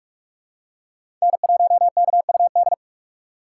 Morse code, heard as a keyed tone, about 46 wpm, 700 Hz. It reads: N1KUD
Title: USA Call Signs - 5 Characters - 6x Character Spacing - 45wpm